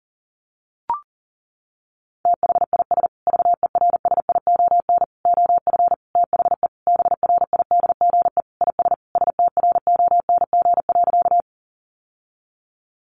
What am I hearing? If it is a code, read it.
Morse code, heard as a keyed tone, about 40 wpm, 700 Hz. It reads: THIS VERSION OF THE BRIDGE IS STRONG.
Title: Sentences from Top 1400 Words 40wpm